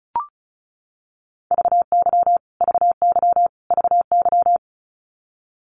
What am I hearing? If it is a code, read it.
Morse code, heard as a keyed tone, about 34 wpm, 700 Hz. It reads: VY VY VY
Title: Abbreviations Repeated 3x - 35wpm